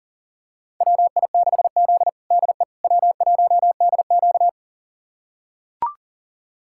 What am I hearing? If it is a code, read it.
Morse code, heard as a keyed tone, about 40 wpm, 700 Hz. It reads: WI6Z DE W1DQ
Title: USA Callsign DE Callsign - 4 Characters - 40wpm